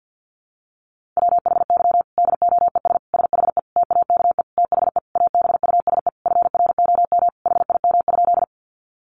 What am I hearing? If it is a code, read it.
Morse code, heard as a keyed tone, about 48 wpm, 700 Hz. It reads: WHY DOES SHE TAKE THE ABUSE FROM HIM?
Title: Sentences from Top 1300 Words 50wpm